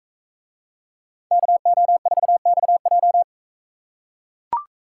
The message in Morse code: KO4XJ

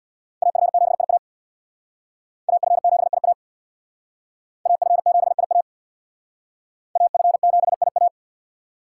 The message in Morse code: AF7IA AF7IA AF7IA AF7IA

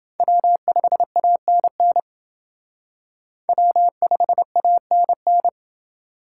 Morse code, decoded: W5ANN W5ANN

